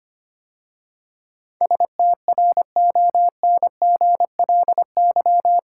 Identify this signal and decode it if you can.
Morse code, heard as a keyed tone, about 25 wpm, 700 Hz. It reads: STRONGLY
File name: 8 Letter Words - First 3 Letters Given - 25wpm